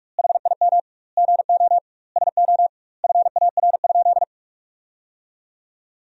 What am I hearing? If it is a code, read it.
Morse code, heard as a keyed tone, about 46 wpm, 700 Hz. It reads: HIM GO SO FAR?
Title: 2x Word-Spacing - Sentences from Top 300 Words - 45wpm